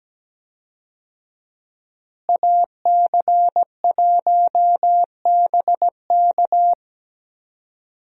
Morse code, heard as a keyed tone, about 17 wpm, 700 Hz. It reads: AC1BK
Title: USA Call Signs - 5 Characters - Repeated 3x - 17wpm